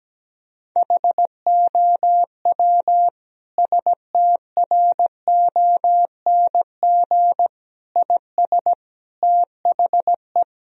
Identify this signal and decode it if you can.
Morse code, heard as a keyed tone, about 17 wpm, 700 Hz. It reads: HOW STRONG IS THE
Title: Sentences from Top 1000 Words - Review Part 4 - 17wpm